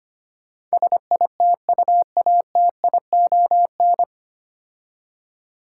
Morse code, heard as a keyed tone, about 25 wpm, 700 Hz. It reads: SITUATION
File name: practice / 25wpm / 9 Letter Words - First 3 Letters Given - 25wpm